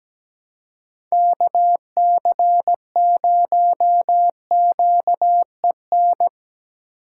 Morse code, heard as a keyed tone, about 19 wpm, 700 Hz. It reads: KC0QEN
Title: Speed Racing - USA Call Signs 17wpm